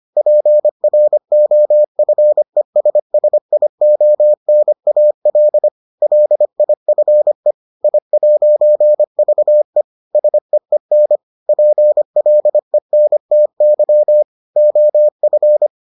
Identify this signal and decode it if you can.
Morse code, heard as a keyed tone, about 25 wpm, 600 Hz. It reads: PROFESSIONAL LIFE I'VE SEEN PLENTY OF